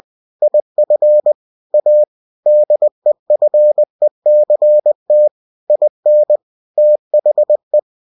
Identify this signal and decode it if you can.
Morse code, heard as a keyed tone, about 20 wpm, 600 Hz. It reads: IF A DEFECT IN THE
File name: cw-books / winnie-the-pooh / winnie-20wpm-page-0117